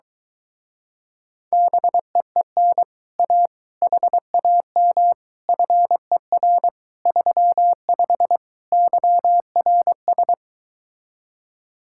Morse code, heard as a keyed tone, about 22 wpm, 700 Hz. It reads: BEEN A HAM FER 35 YRS